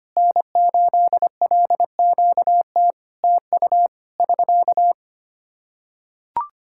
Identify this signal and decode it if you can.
Morse code, heard as a keyed tone, about 25 wpm, 700 Hz. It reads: N8LQT TU <SK>
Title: Practice QSO - Exchange 4 - Ending - 25wpm